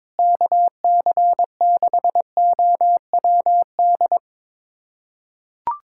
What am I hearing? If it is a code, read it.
Morse code, heard as a keyed tone, about 22 wpm, 700 Hz. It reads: KC6OWD
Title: USA Callsign DE Callsign - 6 Characters - 22wpm